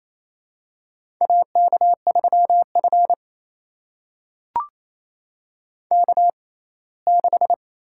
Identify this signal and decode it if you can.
Morse code, heard as a keyed tone, about 28 wpm, spaced as for 19 wpm, 700 Hz. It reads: AK3F K6
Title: USA Call Signs - 4 Characters - 6x Character Spacing - 28wpm